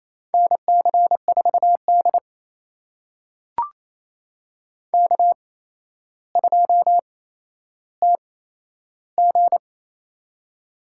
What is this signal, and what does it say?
Morse code, heard as a keyed tone, about 28 wpm, spaced as for 12 wpm, 700 Hz. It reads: NC4D K2TG
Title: USA Call Signs - 8x Character Spacing - 28wpm